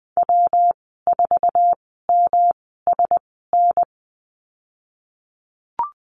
Morse code, heard as a keyed tone, about 20 wpm, spaced as for 15 wpm, 700 Hz. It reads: W4MSN
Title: 122 Lesson 20g - Number 3 - Call Signs 20wpm